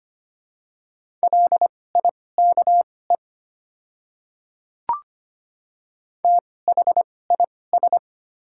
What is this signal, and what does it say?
Morse code, heard as a keyed tone, about 25 wpm, spaced as for 18 wpm, 700 Hz. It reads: LIKE THIS
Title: 2x Character Spacing - Top 100 Words - 25wpm